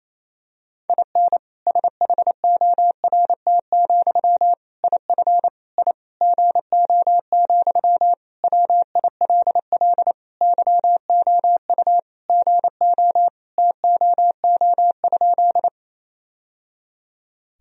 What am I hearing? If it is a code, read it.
Morse code, heard as a keyed tone, about 28 wpm, 700 Hz. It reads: IN SHORT, IF I GO, WILL YOU GO TOO?
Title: Sentences from Top 1000 Words - Review Part 2 - 28wpm